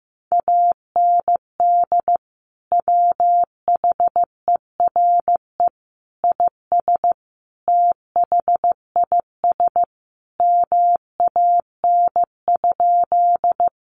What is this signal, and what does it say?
Morse code, heard as a keyed tone, about 15 wpm, 700 Hz. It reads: AND WHERE IS THIS MAN?
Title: Sentences from Top 200 Words 15wpm